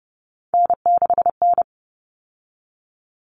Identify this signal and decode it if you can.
Morse code, heard as a keyed tone, about 30 wpm, 700 Hz. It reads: N6N
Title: All Cut Numbers - 3-Digit 30wpm